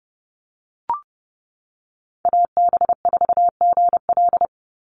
Morse code, heard as a keyed tone, about 30 wpm, 700 Hz. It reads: AB4GL
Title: USA Call Signs 30wpm